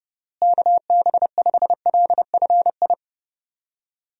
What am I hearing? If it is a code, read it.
Morse code, heard as a keyed tone, about 32 wpm, 700 Hz. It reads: KB5LFI
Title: USA Call Signs - 6 Characters - Repeated 3x - 30wpm